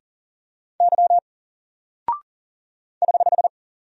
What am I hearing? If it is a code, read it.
Morse code, heard as a keyed tone, about 40 wpm, 700 Hz. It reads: Y <HH>
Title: Single Character (Letter:Number:Punctuation:Prosign) 40wpm